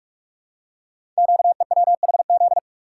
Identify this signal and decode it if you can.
Morse code, heard as a keyed tone, about 46 wpm, 700 Hz. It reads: QEWHZ
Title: Code Groups - 5 Letter:Numbers - 45wpm